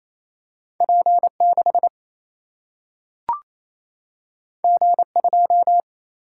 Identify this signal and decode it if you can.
Morse code, heard as a keyed tone, about 28 wpm, 700 Hz. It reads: P6 G2
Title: Single-Letter Single-Number 28wpm